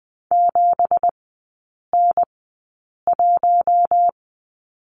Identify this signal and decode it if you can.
Morse code, heard as a keyed tone, about 20 wpm, spaced as for 8 wpm, 700 Hz. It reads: 7N1
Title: 172 Lesson 28e - Number 9 - Three Characters 20wpm